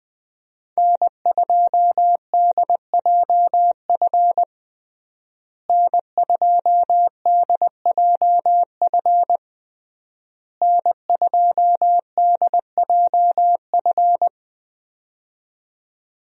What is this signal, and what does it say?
Morse code, heard as a keyed tone, about 20 wpm, 700 Hz. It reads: N2DJF N2DJF N2DJF
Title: USA Call Signs - 5 Characters - Repeated 3x - 20wpm